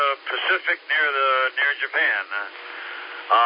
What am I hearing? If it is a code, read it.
Emergency Alert System (SAME) tones, heard as harsh buzzing bursts, with speech. It is heard from 0.4 to 3.1 s.